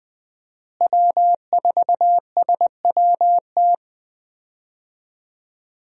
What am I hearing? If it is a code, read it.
Morse code, heard as a keyed tone, about 20 wpm, 700 Hz. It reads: W4SWT
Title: USA Call Signs - 5 Characters - Repeated 2x - 20wpm